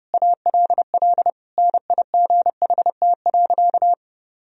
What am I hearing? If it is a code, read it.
Morse code, heard as a keyed tone, about 32 wpm, 700 Hz. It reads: ALL NIGHT.
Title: Sentences from Top 1000 Words - Review Part 2 - 30wpm